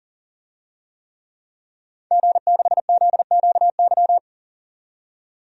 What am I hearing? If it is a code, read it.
Morse code, heard as a keyed tone, about 40 wpm, 700 Hz. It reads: G6ZQY
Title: DXCC-100 Call Signs - 5 Characters - 40wpm